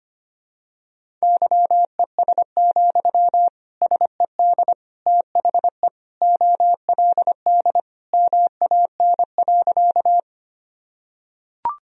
Morse code, heard as a keyed tone, about 25 wpm, 700 Hz. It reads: YES, SED THE OLD MAN.
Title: Sentences from Top 500 Words and Abbreviations 25wpm